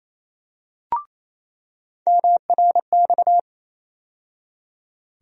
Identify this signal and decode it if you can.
Morse code, heard as a keyed tone, about 28 wpm, 700 Hz. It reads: MRX
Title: USA Call Sign Postfixes - 3 Letters 28wpm